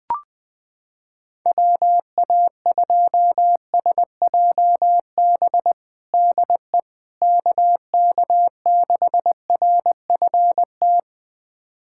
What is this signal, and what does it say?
Morse code, heard as a keyed tone, about 21 wpm, 700 Hz. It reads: WA2SJB DE KK6RFT
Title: USA Callsign DE Callsign - 6 Characters - 20wpm